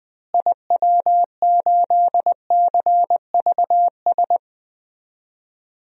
Morse code, heard as a keyed tone, about 20 wpm, 700 Hz. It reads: IW8CVS
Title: DXCC-100 Call Signs - 6 Characters - Repeated 2x - 20wpm